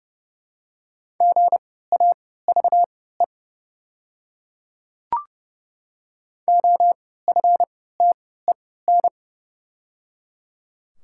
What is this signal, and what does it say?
Morse code, heard as a keyed tone, about 31 wpm, spaced as for 17 wpm, 700 Hz. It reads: GAVE OFTEN
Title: 3x Character Spacing - Top 500 Words - 30wpm